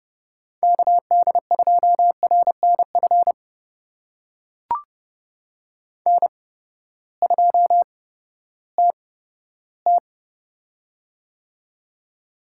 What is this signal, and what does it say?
Morse code, heard as a keyed tone, about 30 wpm, spaced as for 15 wpm, 700 Hz. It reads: KD2RNF N2TT
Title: USA Call Signs - 8x Character Spacing - 30wpm